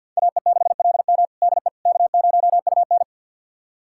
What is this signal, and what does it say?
Morse code, heard as a keyed tone, about 51 wpm, 700 Hz. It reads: AE6LM DE K0UN
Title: USA Callsign DE Callsign 50wpm